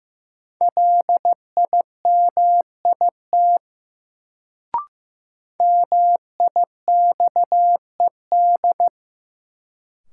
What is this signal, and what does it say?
Morse code, heard as a keyed tone, about 15 wpm, 700 Hz. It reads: LIMIT MIXED
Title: Sentences from Top 1500 Words - New Word Drill - 15wpm